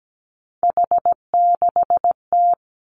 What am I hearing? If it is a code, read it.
Morse code, heard as a keyed tone, about 17 wpm, 700 Hz. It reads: H6T
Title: Three Challenging Letter-Number Combinations 17wpm